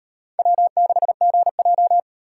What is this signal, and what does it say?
Morse code, heard as a keyed tone, about 38 wpm, 700 Hz. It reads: W6GJ